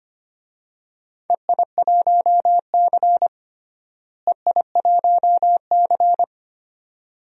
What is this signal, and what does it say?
Morse code, heard as a keyed tone, about 25 wpm, 700 Hz. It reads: EI1C EI1C
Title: DXCC-100 Call Signs - 4 Characters - Repeated 2x - 25wpm